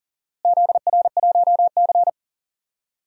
Morse code, heard as a keyed tone, about 40 wpm, 700 Hz. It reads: ZR1C
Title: DXCC-100 Call Signs - 4 Characters - Repeated 3x - 40wpm